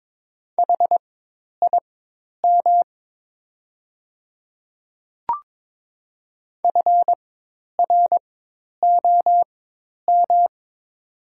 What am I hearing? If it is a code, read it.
Morse code, heard as a keyed tone, about 22 wpm, spaced as for 10 wpm, 700 Hz. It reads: HIM FROM
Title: 4x Character Spacing - Top 300 Words - 22wpm